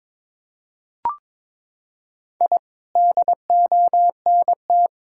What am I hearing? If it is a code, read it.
Morse code, heard as a keyed tone, about 22 wpm, 700 Hz. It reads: I DONT
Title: Sentences from Top 1000 Words - Review Part 6 - 22wpm